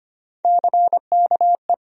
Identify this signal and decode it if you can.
Morse code, heard as a keyed tone, about 25 wpm, 700 Hz. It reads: CKE
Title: USA Call Sign Postfixes - 3 Letters 25wpm